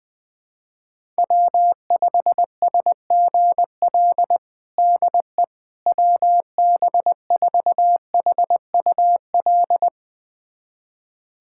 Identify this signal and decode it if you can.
Morse code, heard as a keyed tone, about 21 wpm, 700 Hz. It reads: W5SGL DE WB4HUL
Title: USA Callsign DE Callsign 20wpm